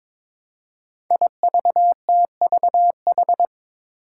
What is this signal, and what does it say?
Morse code, heard as a keyed tone, about 22 wpm, 700 Hz. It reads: IVTVH
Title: Five Challenging Letter-Number Combinations 22wpm